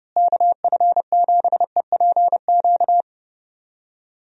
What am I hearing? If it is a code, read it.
Morse code, heard as a keyed tone, about 30 wpm, 700 Hz. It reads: KF7EPQ
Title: USA Call Signs - 6 Characters - Repeated 2x - 30wpm